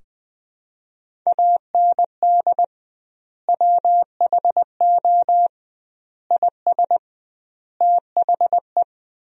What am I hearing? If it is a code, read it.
Morse code, heard as a keyed tone, about 20 wpm, 700 Hz. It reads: AND WHO IS THE
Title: 2x Word-Spacing - Sentences from Top 200 Words - 20wpm